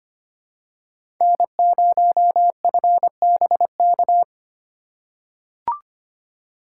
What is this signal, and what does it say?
Morse code, heard as a keyed tone, about 25 wpm, 700 Hz. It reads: N0FBK
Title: USA Call Signs - 5 Characters - 4x Character Spacing - 25wpm